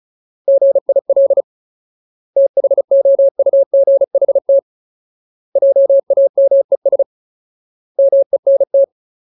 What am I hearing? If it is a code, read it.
Morse code, heard as a keyed tone, about 34 wpm, 550 Hz. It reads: GIL THOUGHT JAMES MENT
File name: Ultimate Word List / 35WPM Ultimate